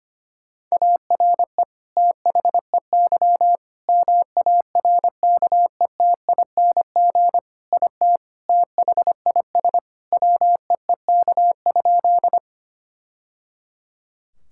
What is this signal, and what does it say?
Morse code, heard as a keyed tone, about 25 wpm, 700 Hz. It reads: ARE THEY MARKETING IT THIS WEEK?
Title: Sentences from Top 1000 Words - Review Part 5 - 25wpm